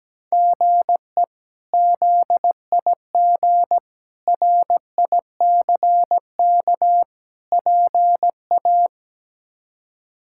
Morse code, heard as a keyed tone, about 17 wpm, 700 Hz. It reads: GE ZIG RICK PA
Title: K1USN SST QSO Practice - 17wpm